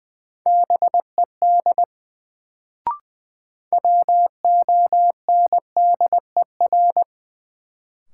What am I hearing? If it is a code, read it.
Morse code, heard as a keyed tone, about 20 wpm, 700 Hz. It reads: BED WONDER
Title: Sentences from Top 500 Words - New Word Drill - 20wpm